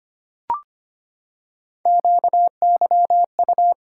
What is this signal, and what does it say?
Morse code, heard as a keyed tone, about 25 wpm, 700 Hz. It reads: QYU
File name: Three Letter-Number Combinations - 25wpm